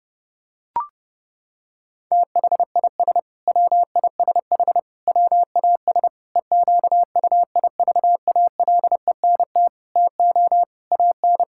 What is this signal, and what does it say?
Morse code, heard as a keyed tone, about 32 wpm, 700 Hz. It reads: THIS WISH WAS EQUIVALENT TO AN
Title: Sentences from Top 1600 Words 30wpm